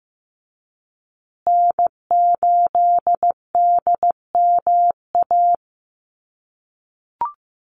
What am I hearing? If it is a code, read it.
Morse code, heard as a keyed tone, about 15 wpm, 700 Hz. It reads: N8DMA
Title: USA Call Signs Repeated 3x 15wpm